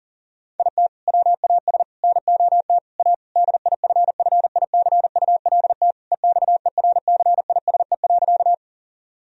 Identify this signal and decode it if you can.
Morse code, heard as a keyed tone, about 37 wpm, 700 Hz. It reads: IT WAS NOT A DIFFICULT EXERCISE.
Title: Sentences from Top 1000 Words - Review Part 4 - 40wpm